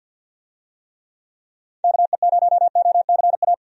Morse code, heard as a keyed tone, about 48 wpm, 700 Hz. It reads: KE0OCA
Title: USA Call Signs - 6 Characters - 50wpm